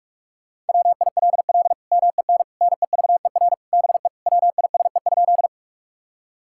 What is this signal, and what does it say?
Morse code, heard as a keyed tone, about 42 wpm, 700 Hz. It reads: WILL MEN NEVER BE WISE?
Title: Sentences from Top 1600 Words 45wpm